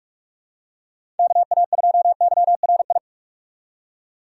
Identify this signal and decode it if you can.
Morse code, heard as a keyed tone, about 47 wpm, 700 Hz. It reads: KA2YRI
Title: USA Call Signs - 6 Characters - 4x Character Spacing - 45wpm